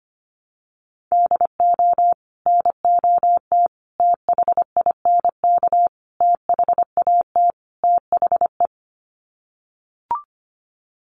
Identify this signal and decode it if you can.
Morse code, heard as a keyed tone, about 25 wpm, 700 Hz. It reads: DO NOT THINK THAT THE
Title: Sets of 5 Words Encore2 25wpm